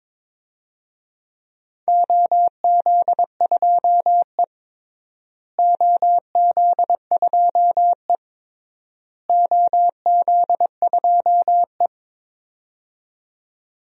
Morse code, heard as a keyed tone, about 22 wpm, 700 Hz. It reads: OZ2E OZ2E OZ2E
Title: DXCC-100 Call Signs - 4 Characters - Repeated 3x - 22wpm